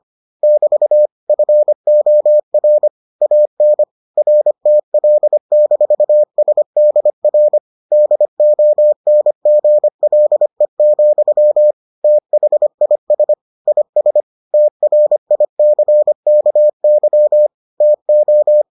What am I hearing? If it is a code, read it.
Morse code, heard as a keyed tone, about 25 wpm, 600 Hz. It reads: <BT> FOR AN RTL-SDR DONGLE, THIS IS TRICKY TO